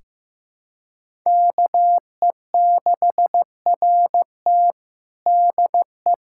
Morse code, heard as a keyed tone, about 15 wpm, 700 Hz. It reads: KE6RT DE
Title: USA Callsign DE Callsign 15wpm